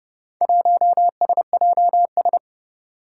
Morse code, heard as a keyed tone, about 32 wpm, 700 Hz. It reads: 1SJS
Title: Four Challenging Letter-Number Combinations 30wpm